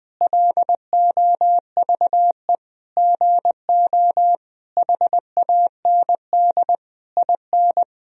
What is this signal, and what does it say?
Morse code, heard as a keyed tone, about 20 wpm, 700 Hz. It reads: LOVE GO HAND IN
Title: Sentences from Top 1000 Words - Review Part 3 - 20wpm